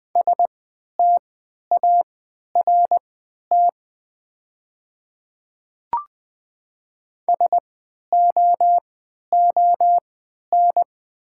Morse code, heard as a keyed tone, about 20 wpm, spaced as for 11 wpm, 700 Hz. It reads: START SOON
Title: 3x Character Spacing - Top 500 Words - 20wpm